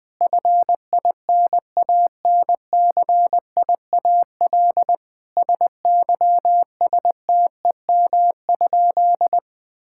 Morse code, heard as a keyed tone, about 21 wpm, 700 Hz. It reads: FINANCIAL SYSTEM?
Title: Sentences from Top 1000 Words - Review Part 5 - 20wpm